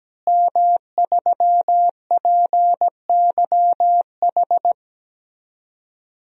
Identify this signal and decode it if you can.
Morse code, heard as a keyed tone, about 17 wpm, 700 Hz. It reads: M3PYH
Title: DXCC-100 Call Signs - 5 Characters - Repeated 3x - 17wpm